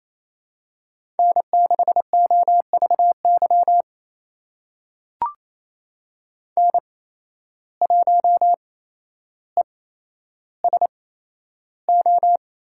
Morse code, heard as a keyed tone, about 28 wpm, spaced as for 12 wpm, 700 Hz. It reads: N6OVY N1ESO
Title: USA Call Signs - 5 Characters - 8x Character Spacing - 28wpm